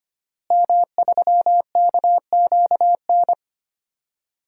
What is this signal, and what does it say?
Morse code, heard as a keyed tone, about 25 wpm, 700 Hz. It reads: M3KQN
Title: DXCC-100 Call Signs - 5 Characters - Repeated 3x - 25wpm